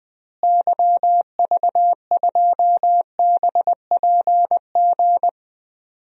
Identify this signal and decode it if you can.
Morse code, heard as a keyed tone, about 20 wpm, 700 Hz. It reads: YV2BPG
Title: DXCC-100 Call Signs - 6 Characters - Repeated 2x - 20wpm